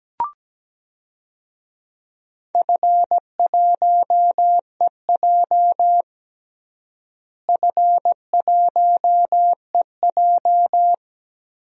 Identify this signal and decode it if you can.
Morse code, heard as a keyed tone, about 17 wpm, 700 Hz. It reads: F1EJ F1EJ
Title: DXCC-100 Call Signs - 4 Characters - Repeated 3x - 17wpm